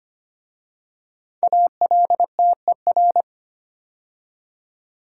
Morse code, heard as a keyed tone, about 25 wpm, 700 Hz. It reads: ALTER
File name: 5 Letter Words - First 3 Letters Given - 25wpm